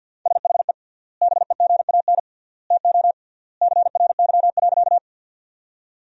Morse code, heard as a keyed tone, about 53 wpm, 700 Hz. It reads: SHE BEGAN TO CRY.
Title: 3x Word-Spacing - Sentences from Top 500 Words - 50wpm